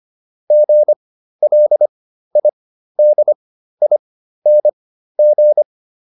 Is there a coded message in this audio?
Yes: Morse code from 0.5 to 5.6 s, heard as a keyed tone, about 25 wpm, spaced as for 13 wpm, 600 Hz. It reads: GLIDING